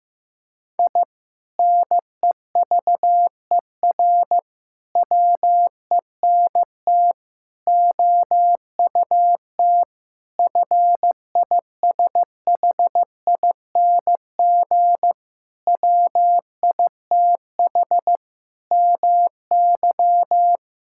Morse code, heard as a keyed tone, about 15 wpm, 700 Hz. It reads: I NEVER WENT OUT FISHING WITH MY
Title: Sentences from Top 1000 Words - Review Part 5 - 15wpm